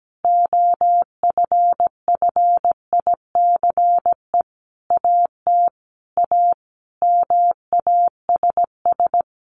Morse code, heard as a keyed tone, about 17 wpm, 700 Hz. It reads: OFFICE AT A MASS
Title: A Sound Of Thunder 17wpm